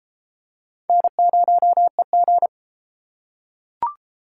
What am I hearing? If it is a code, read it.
Morse code, heard as a keyed tone, about 31 wpm, 700 Hz. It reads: N0EG